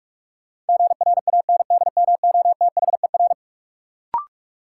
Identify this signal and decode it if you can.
Morse code, heard as a keyed tone, about 41 wpm, 700 Hz. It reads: GRANDMOTHER
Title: Sentences from Top 1200 Words - New Word Drill Alt - 45wpm